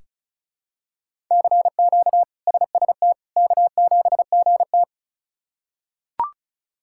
Morse code, heard as a keyed tone, about 34 wpm, 700 Hz. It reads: CQ SST K7GT
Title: K1USN SST QSO Practice - 35wpm